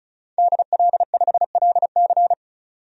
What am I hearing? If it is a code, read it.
Morse code, heard as a keyed tone, about 36 wpm, 700 Hz. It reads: DL5LC
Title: DXCC-100 Call Signs - 5 Characters - 35wpm